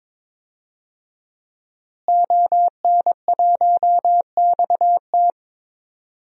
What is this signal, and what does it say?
Morse code, heard as a keyed tone, about 22 wpm, 700 Hz. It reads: ON1XT
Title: DXCC-100 Call Signs - 5 Characters - Repeated 2x - 22wpm